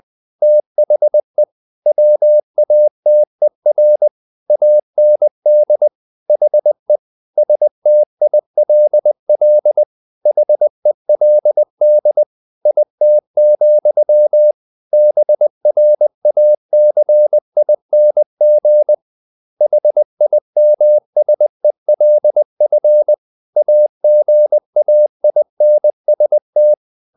Morse code, heard as a keyed tone, about 20 wpm, 600 Hz. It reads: THE WATER AND HE STILL HELD IT, BRACING HIMSELF AGAINST
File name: the old man and the sea / the old man and the sea part 1 20wpm